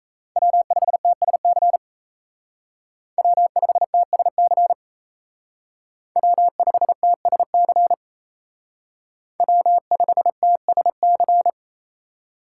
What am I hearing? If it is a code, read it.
Morse code, heard as a keyed tone, about 34 wpm, 700 Hz. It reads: W5TSC W5TSC W5TSC W5TSC
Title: Speed Racing - USA 1x3 Call Signs 28wpm